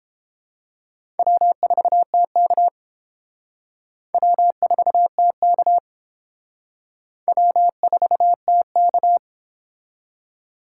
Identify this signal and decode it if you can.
Morse code, heard as a keyed tone, about 30 wpm, 700 Hz. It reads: W4TK W4TK W4TK